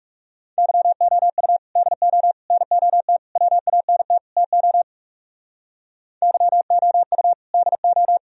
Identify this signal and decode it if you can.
Morse code, heard as a keyed tone, about 36 wpm, 700 Hz. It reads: YOU DO NOT WANT TO YOU DO
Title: Speed-Racing - Sets of 5 Words 30wpm